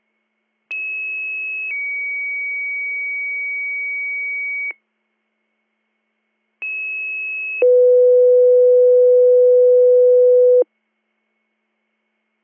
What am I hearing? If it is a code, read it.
Two-tone page, heard as a one-second tone, then a three-second tone, about 2600 then 2250 Hz, about 2600 then 500 Hz. It is heard from 0.7 to 10.6 s.